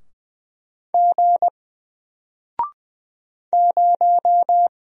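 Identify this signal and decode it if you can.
Morse code, heard as a keyed tone, about 20 wpm, 700 Hz. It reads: G 0
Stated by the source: Single letter and number Morse drill at 20 wpm, from N4LCD.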